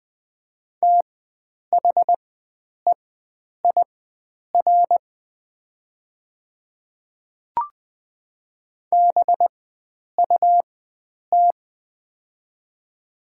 Morse code, heard as a keyed tone, about 20 wpm, spaced as for 9 wpm, 700 Hz. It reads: THEIR BUT
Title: 4x Character Spacing - Top 100 Words - 20wpm